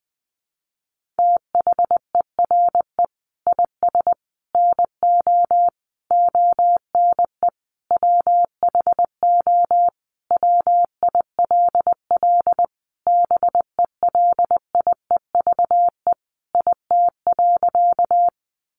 Morse code, heard as a keyed tone, about 20 wpm, 700 Hz. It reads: THERE IS NO ONE WHO WILL BELIEVE IT.
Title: Sentences from Top 1000 Words 20wpm